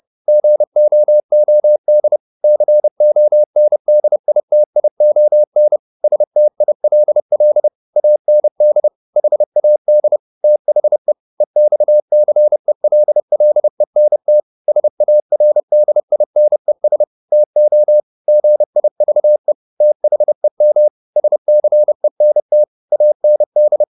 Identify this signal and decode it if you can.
Morse code, heard as a keyed tone, about 30 wpm, 600 Hz. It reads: GOOD CONDITION STILL AND HAD THE EXCELLENT SARDINES TO GIVE THEM SCENT AND
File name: the old man and the sea / the old man and the sea 1 30wpm